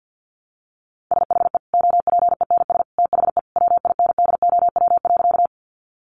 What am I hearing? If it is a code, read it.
Morse code, heard as a keyed tone, about 50 wpm, 700 Hz. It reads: SHE OPENS THE WINDOW.
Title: Easy Phrases 50wpm